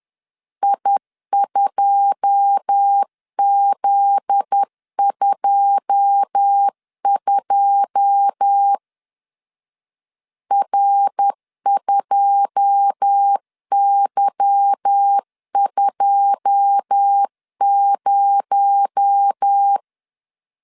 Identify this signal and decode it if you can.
Morse code, heard as a keyed tone, about 10 wpm, 800 Hz. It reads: I2Z22 R2Y20
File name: LEZIONI / 9 2 10wpm 2736